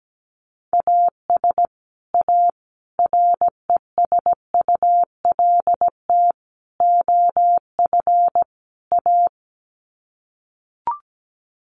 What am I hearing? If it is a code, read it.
Morse code, heard as a keyed tone, about 17 wpm, 700 Hz. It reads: AS A RESULT OF A
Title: Sets of 5 Words Encore 17wpm